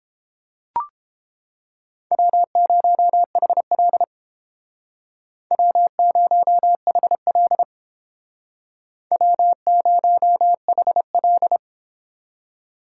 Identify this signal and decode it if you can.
Morse code, heard as a keyed tone, about 30 wpm, 700 Hz. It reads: W0HL W0HL W0HL